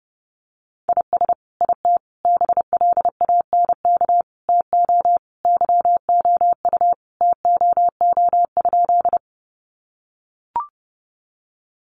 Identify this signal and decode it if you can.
Morse code, heard as a keyed tone, about 28 wpm, 700 Hz. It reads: IS IT BLANK TO YOU TOO?
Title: Sentences from Top 1500 Words 30wpm